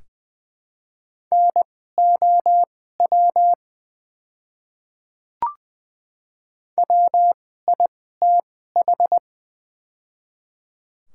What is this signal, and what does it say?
Morse code, heard as a keyed tone, about 20 wpm, spaced as for 14 wpm, 700 Hz. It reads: NOW WITH
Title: 2x Character Spacing - Top 100 Words - 20wpm